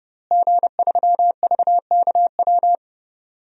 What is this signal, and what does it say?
Morse code, heard as a keyed tone, about 30 wpm, 700 Hz. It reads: G3VKW
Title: DXCC-100 Call Signs - 5 Characters - Repeated 2x - 30wpm